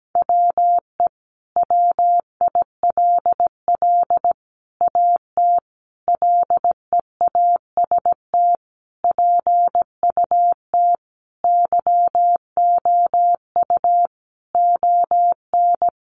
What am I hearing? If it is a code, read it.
Morse code, heard as a keyed tone, about 17 wpm, 700 Hz. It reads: WE WILL AT LEAST PUT YOU ON
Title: Sentences from Top 1200 Words 17wpm